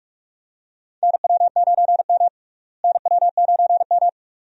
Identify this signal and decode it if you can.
Morse code, heard as a keyed tone, about 43 wpm, 700 Hz. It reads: NW9M NW9M
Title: USA Call Signs - 4 Characters - Repeated 2x - 45wpm